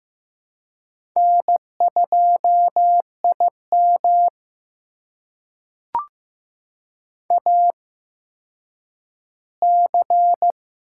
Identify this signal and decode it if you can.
Morse code, heard as a keyed tone, about 15 wpm, spaced as for 9 wpm, 700 Hz. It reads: N2IM AC
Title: USA Call Signs - 8x Character Spacing - 15wpm